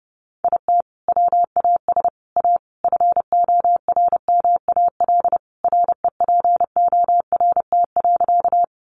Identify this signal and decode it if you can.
Morse code, heard as a keyed tone, about 30 wpm, 700 Hz. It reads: IT WAS A FORMAL REPORT.
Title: Sentences from Top 1300 Words 30wpm